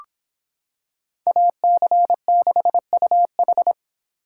Morse code, heard as a keyed tone, about 26 wpm, 700 Hz. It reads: AC6UH